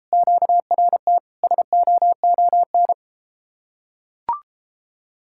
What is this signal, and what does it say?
Morse code, heard as a keyed tone, about 33 wpm, 700 Hz. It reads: QRT SOON